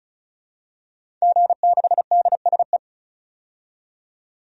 Morse code, heard as a keyed tone, about 34 wpm, 700 Hz. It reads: G6DSE
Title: DXCC-100 Call Signs - 5 Characters - 35wpm